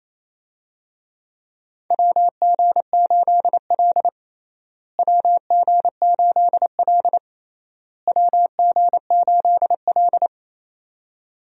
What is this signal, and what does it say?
Morse code, heard as a keyed tone, about 28 wpm, 700 Hz. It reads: WG8L WG8L WG8L
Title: USA Call Signs - 4 Characters - Repeated 3x - 28wpm